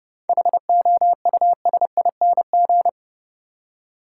Morse code, heard as a keyed tone, about 29 wpm, 700 Hz. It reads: HOUSING
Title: Seven Letter Words - 30wpm